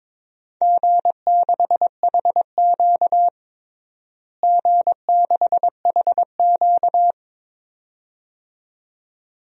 Morse code, heard as a keyed tone, about 22 wpm, 700 Hz. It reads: G6HQ G6HQ
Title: DXCC-100 Call Signs - 4 Characters - Repeated 2x - 22wpm